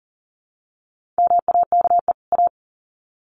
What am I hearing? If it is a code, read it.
Morse code, heard as a keyed tone, about 40 wpm, 700 Hz. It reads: MAKE A
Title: Sets of 2 Words 40wpm